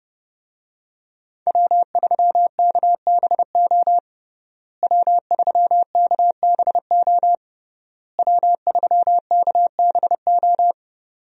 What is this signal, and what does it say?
Morse code, heard as a keyed tone, about 30 wpm, 700 Hz. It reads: W3KBO W3KBO W3KBO
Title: USA Call Signs - 5 Characters - Repeated 3x - 30wpm